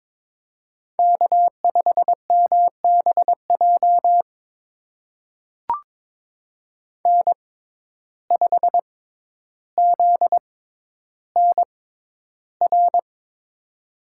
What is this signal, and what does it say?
Morse code, heard as a keyed tone, about 22 wpm, spaced as for 11 wpm, 700 Hz. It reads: K5MBJ N5ZNR
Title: USA Call Signs - 5 Characters - 6x Character Spacing - 22wpm